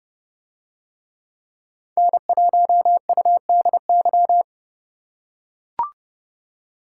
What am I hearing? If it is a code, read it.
Morse code, heard as a keyed tone, about 31 wpm, 700 Hz. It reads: N1UDY